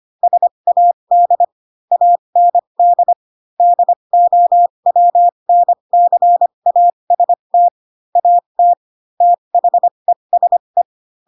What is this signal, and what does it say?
Morse code, heard as a keyed tone, about 25 wpm, spaced as for 22 wpm, 700 Hz. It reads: SAD AND DOWNCAST AT THESE